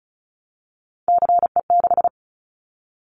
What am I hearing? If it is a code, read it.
Morse code, heard as a keyed tone, about 36 wpm, 700 Hz. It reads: CE6
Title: Callsign Prefixes for Top 100 DXCC - 35wpm